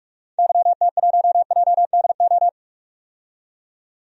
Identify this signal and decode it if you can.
Morse code, heard as a keyed tone, about 44 wpm, 700 Hz. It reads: YT1JDO
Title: DXCC-100 Call Signs - 6 Characters - 45wpm